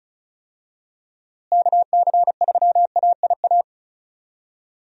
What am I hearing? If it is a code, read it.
Morse code, heard as a keyed tone, about 36 wpm, 700 Hz. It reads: KC3AIA
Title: USA Call Signs - 6 Characters - Repeated 3x - 35wpm